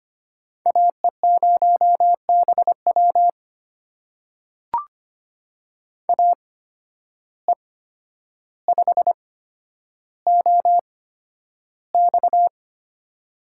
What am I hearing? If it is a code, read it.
Morse code, heard as a keyed tone, about 25 wpm, spaced as for 11 wpm, 700 Hz. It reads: AE0BW AE5OX
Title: USA Call Signs - 5 Characters - 8x Character Spacing - 25wpm